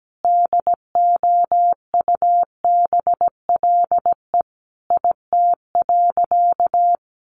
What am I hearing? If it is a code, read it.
Morse code, heard as a keyed tone, about 17 wpm, 700 Hz. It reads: DOUBLE IT.
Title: Sentences from Top 1400 Words 17wpm